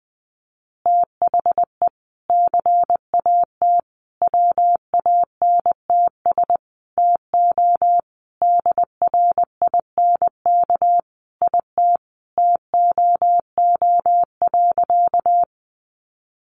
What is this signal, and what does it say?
Morse code, heard as a keyed tone, about 20 wpm, 700 Hz. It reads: THE CAT WANTS TO DRINK IT TOO.
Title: Additional Easy Phrases 20wpm